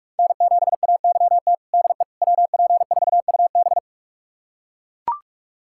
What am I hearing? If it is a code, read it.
Morse code, heard as a keyed tone, about 41 wpm, 700 Hz. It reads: N7AYT DE WP4UB
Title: USA Callsign DE Callsign - 5 Characters - 45wpm